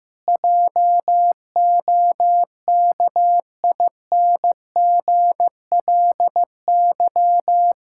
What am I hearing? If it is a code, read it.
Morse code, heard as a keyed tone, about 15 wpm, 700 Hz. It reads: JOKINGLY
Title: Pangram Challenge v2 - 15wpm